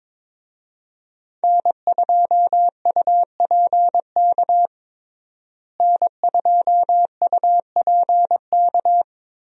Morse code, heard as a keyed tone, about 23 wpm, 700 Hz. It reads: N2UPK N2UPK
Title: USA Call Signs - 5 Characters - Repeated 3x - 22wpm